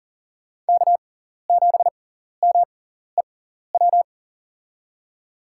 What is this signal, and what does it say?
Morse code, heard as a keyed tone, about 40 wpm, spaced as for 14 wpm, 700 Hz. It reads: K7MEW
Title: USA Call Signs - 6x Character Spacing - 40wpm